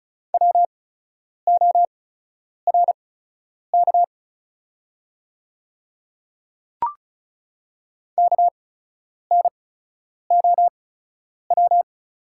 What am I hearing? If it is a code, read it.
Morse code, heard as a keyed tone, about 36 wpm, spaced as for 10 wpm, 700 Hz. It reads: WORK KNOW
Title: 8x Character Spacing - Top 300 Words - 35wpm